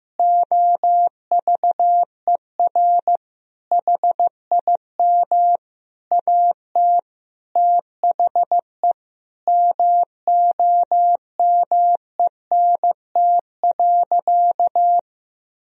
Morse code, heard as a keyed tone, about 15 wpm, 700 Hz. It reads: OVER HIM AT THE MOMENT.
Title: Sentences from Top 1000 Words - Review Part 3 - 15wpm